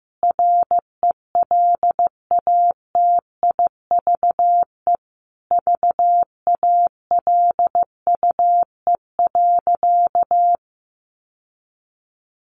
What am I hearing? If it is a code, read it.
Morse code, heard as a keyed tone, about 15 wpm, 700 Hz. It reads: RELATIVE VALUE.
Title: Sentences from Top 1400 Words 15wpm